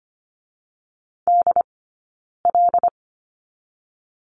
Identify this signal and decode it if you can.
Morse code, heard as a keyed tone, about 25 wpm, spaced as for 9 wpm, 700 Hz. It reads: DL
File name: Lesson 17d - Letter u - Two Characters 25wpm